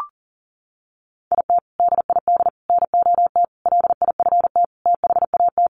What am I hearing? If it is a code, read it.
Morse code, heard as a keyed tone, about 40 wpm, 700 Hz. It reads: IT DID NOT LIFT THAT
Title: Sentences from Top 1500 Words 40wpm